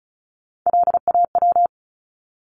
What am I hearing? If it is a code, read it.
Morse code, heard as a keyed tone, about 34 wpm, 700 Hz. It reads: LAW
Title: Top 501-750 Words 35wpm